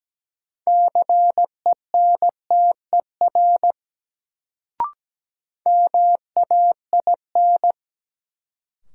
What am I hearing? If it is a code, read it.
Morse code, heard as a keyed tone, about 17 wpm, 700 Hz. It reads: CENTER MAIN
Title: Sentences from Top 1000 Words - Review Part 2 - New Word Drill - 17wpm